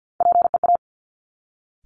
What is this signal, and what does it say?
Morse code, heard as a keyed tone, about 50 wpm, 700 Hz. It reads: PEA
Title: Top Three Letter Combinations in Words 50wpm